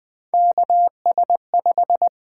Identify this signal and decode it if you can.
Morse code, heard as a keyed tone, about 20 wpm, 700 Hz. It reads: KS5
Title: Three Letter-Number Combinations - 20wpm